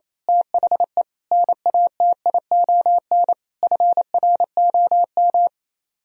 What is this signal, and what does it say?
Morse code, heard as a keyed tone, about 26 wpm, 700 Hz. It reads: THE NATION FROM
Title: Speed-Racing - Sets of 5 Words 28wpm